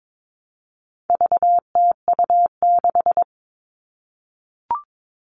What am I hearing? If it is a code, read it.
Morse code, heard as a keyed tone, about 22 wpm, 700 Hz. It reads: VTU6
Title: All Cut Numbers - 4-Digit 22wpm